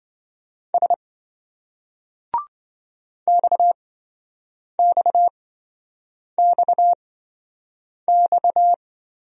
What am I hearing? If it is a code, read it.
Morse code, heard as a keyed tone, about 28 wpm, spaced as for 7 wpm, 700 Hz. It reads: S XXXX